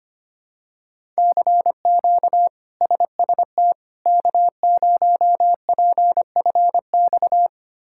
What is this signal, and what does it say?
Morse code, heard as a keyed tone, about 25 wpm, 700 Hz. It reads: CQ SST K0PFX
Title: K1USN SST QSO Practice - 25wpm